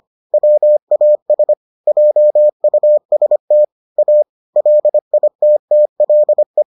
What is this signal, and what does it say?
Morse code, heard as a keyed tone, about 25 wpm, 600 Hz. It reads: WAS JUST A LITTLE